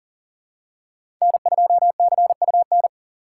Morse code, heard as a keyed tone, about 40 wpm, 700 Hz. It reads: N2CUN
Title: USA Call Signs - 5 Characters - 6x Character Spacing - 40wpm